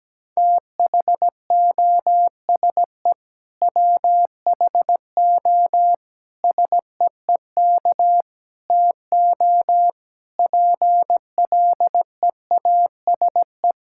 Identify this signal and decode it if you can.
Morse code, heard as a keyed tone, about 17 wpm, 700 Hz. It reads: THOSE WHO SEEK TO PLEASE
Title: Sentences from Top 1600 Words 17wpm